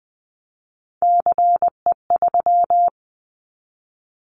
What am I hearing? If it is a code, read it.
Morse code, heard as a keyed tone, about 20 wpm, 700 Hz. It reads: CE3
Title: Callsign Prefixes for Top 100 DXCC - 20wpm